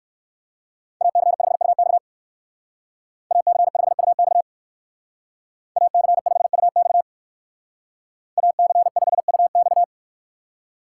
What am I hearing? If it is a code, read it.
Morse code, heard as a keyed tone, about 58 wpm, 700 Hz. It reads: AC5UX AC5UX AC5UX AC5UX